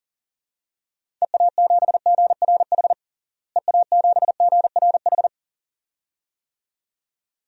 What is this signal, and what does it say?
Morse code, heard as a keyed tone, about 40 wpm, 700 Hz. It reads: EA7GRH EA7GRH
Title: DXCC-100 Call Signs - 6 Characters - Repeated 2x - 40wpm